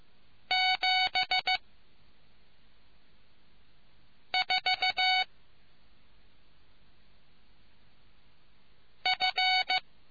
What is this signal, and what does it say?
Morse code, heard as a keyed tone, about 15 wpm, 750 Hz. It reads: 7 4 F